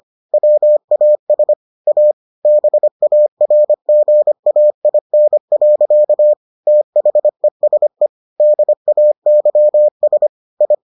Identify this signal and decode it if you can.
Morse code, heard as a keyed tone, about 25 wpm, 600 Hz. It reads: WAS A BARGAIN. THESE DAYS I